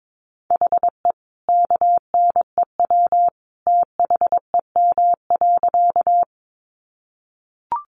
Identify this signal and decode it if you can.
Morse code, heard as a keyed tone, about 22 wpm, 700 Hz. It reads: HE KNEW THEM.
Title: Sentences from Top 400 Words 22wpm